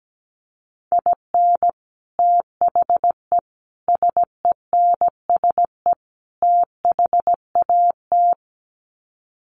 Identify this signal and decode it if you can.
Morse code, heard as a keyed tone, about 17 wpm, 700 Hz. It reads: IN THE SENSE THAT
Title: Sets of 4 Words 17wpm